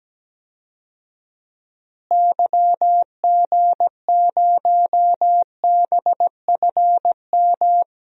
Morse code, heard as a keyed tone, about 17 wpm, 700 Hz. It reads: YG0BFM
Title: DXCC-100 Call Signs - 6 Characters - Repeated 2x - 17wpm